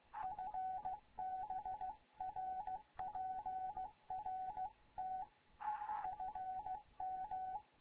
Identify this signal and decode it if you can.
Morse code, heard as a keyed tone, about 15 wpm, 750 Hz, with radio noise. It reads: FB RPRT FM
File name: Morze-HA2MN / hangp / 2 11 GYAKORLAT